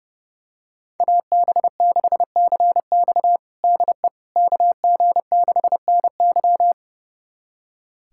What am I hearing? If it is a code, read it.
Morse code, heard as a keyed tone, about 32 wpm, 700 Hz. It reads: AB6CX DE KG6NY
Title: USA Callsign DE Callsign - 5 Characters - 30wpm